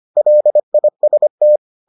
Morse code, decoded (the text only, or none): LIST